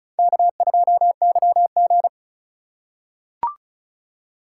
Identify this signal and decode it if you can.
Morse code, heard as a keyed tone, about 36 wpm, 700 Hz. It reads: K2YG
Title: USA Call Signs - 4 Characters - 3x Character Spacing - 35wpm